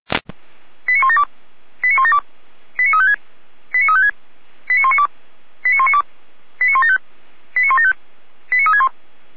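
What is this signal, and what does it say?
Five-tone selective call (ZVEI), heard as quick runs of five tones, tones of about 80 ms. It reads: 89172 89172 89356 89356 89192 89192 89175 89175 89361
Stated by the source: ZVEI five-tone selective call received on an SDR, hosted by fire brigade Feuerwehr Orlamünde.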